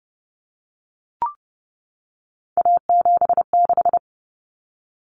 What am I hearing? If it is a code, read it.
Morse code, heard as a keyed tone, about 30 wpm, 700 Hz. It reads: A76
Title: Common Cut Numbers - 3-Digit 30wpm